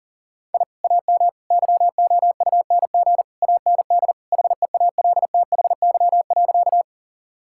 Morse code, heard as a keyed tone, about 40 wpm, 700 Hz. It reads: I AM YOUNG AND HEALTHY.
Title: Sentences from Top 1000 Words - Review Part 5 - 40wpm